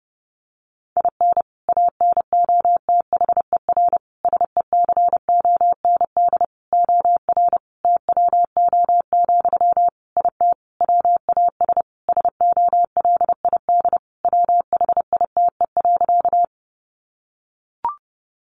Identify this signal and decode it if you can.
Morse code, heard as a keyed tone, about 30 wpm, 700 Hz. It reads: IN ANOTHER SECOND OR TWO, IT WAS SOLID WHITE.
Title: Sentences from Top 1500 Words 30wpm